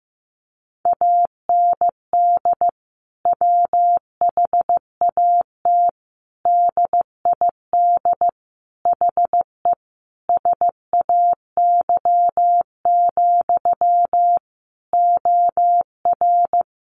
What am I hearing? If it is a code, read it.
Morse code, heard as a keyed tone, about 15 wpm, 700 Hz. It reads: AND WHAT DID HE SAY, OR
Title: Sentences from Top 700 Words 15wpm